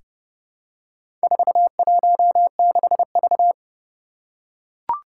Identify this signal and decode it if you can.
Morse code, heard as a keyed tone, about 28 wpm, 700 Hz. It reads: 416V
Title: Four Challenging Letter-Number Combinations 30wpm